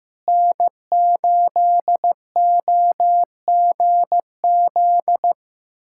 Morse code, heard as a keyed tone, about 16 wpm, 700 Hz. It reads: N8OGZ
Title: USA Callsign DE Callsign - 5 Characters - 15wpm